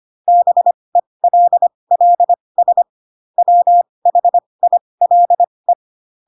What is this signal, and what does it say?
Morse code, heard as a keyed tone, about 25 wpm, spaced as for 20 wpm, 700 Hz. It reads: BELLS WHILE
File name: Pinocchio / Pinocchio ch6 15 25 ik1hge